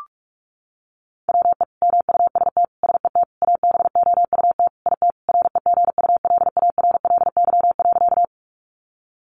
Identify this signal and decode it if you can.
Morse code, heard as a keyed tone, about 44 wpm, 700 Hz. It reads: WE MUST SET ABOUT IT REGULARLY.
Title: Sentences from Top 1400 Words 45wpm